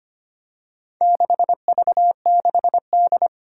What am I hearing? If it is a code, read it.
Morse code, heard as a keyed tone, about 25 wpm, 700 Hz. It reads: 6V6D
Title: Four Challenging Letter-Number Combinations 25wpm